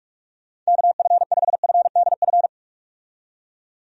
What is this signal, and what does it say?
Morse code, heard as a keyed tone, about 44 wpm, 700 Hz. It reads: KF5FDF
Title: USA Call Signs - 6 Characters - 45wpm